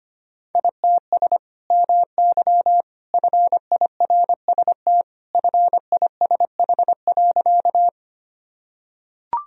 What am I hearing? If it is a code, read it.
Morse code, heard as a keyed tone, about 25 wpm, 700 Hz. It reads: ITS MY FIRST FISH.
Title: Sentences from Top 1000 Words - Review Part 1 - 25wpm